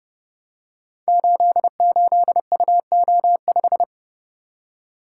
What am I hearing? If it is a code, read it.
Morse code, heard as a keyed tone, about 30 wpm, 700 Hz. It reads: 88UO5
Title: Code Groups - 5 Letter:Numbers - 30wpm